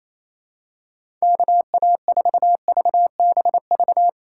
Morse code, heard as a keyed tone, about 28 wpm, 700 Hz. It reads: KA4VBV
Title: USA Call Signs - 6 Characters - 4x Character Spacing - 28wpm